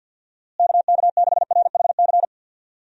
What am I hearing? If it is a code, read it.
Morse code, heard as a keyed tone, about 52 wpm, 700 Hz. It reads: KK6RHC
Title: USA Call Signs - 6 Characters - Repeated 3x - 50wpm